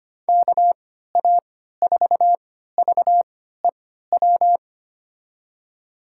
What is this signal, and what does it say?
Morse code, heard as a keyed tone, about 25 wpm, spaced as for 14 wpm, 700 Hz. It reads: KA4VEW
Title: USA Call Signs - 6 Characters - 3x Character Spacing - 25wpm